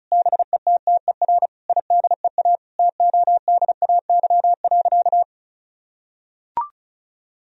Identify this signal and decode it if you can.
Morse code, heard as a keyed tone, about 36 wpm, 700 Hz. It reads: BETTER IDEA TODAY.
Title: Sentences from Top 1000 Words - Review Part 4 - 35wpm